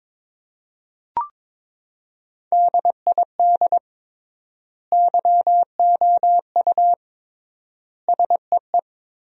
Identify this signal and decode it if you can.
Morse code, heard as a keyed tone, about 22 wpm, 700 Hz. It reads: DID YOU SEE
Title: 3x Word-Spacing - Sentences from Top 100 Words - 22wpm